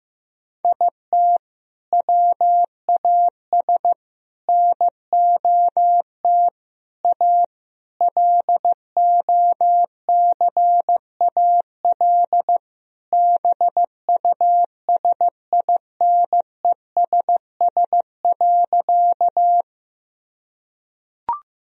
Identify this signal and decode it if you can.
Morse code, heard as a keyed tone, about 15 wpm, 700 Hz. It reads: IT WAS NOT A LOCAL BUSINESS.
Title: Sentences from Top 1000 Words - Review Part 3 - 15wpm